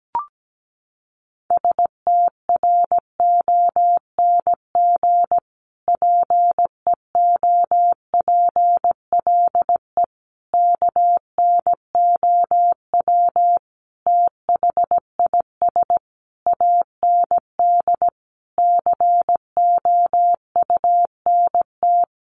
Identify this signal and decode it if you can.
Morse code, heard as a keyed tone, about 17 wpm, 700 Hz. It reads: STRONG PEOPLE KNOW THIS AND COUNT
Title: Sentences from Top 1500 Words 17wpm